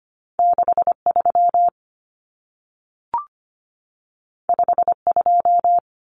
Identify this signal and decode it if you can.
Morse code, heard as a keyed tone, about 25 wpm, 700 Hz. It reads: 63 52
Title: Numbers - 2-Digit 25wpm